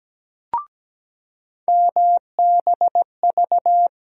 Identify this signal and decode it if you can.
Morse code, heard as a keyed tone, about 17 wpm, 700 Hz. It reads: MBV
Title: USA Call Sign Postfixes - 3 Letters 17wpm